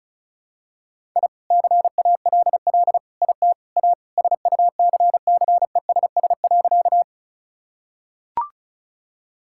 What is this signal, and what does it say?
Morse code, heard as a keyed tone, about 36 wpm, 700 Hz. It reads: I CALL IT A SUCCESS.
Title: Sentences from Top 1000 Words - Review Part 6 - 35wpm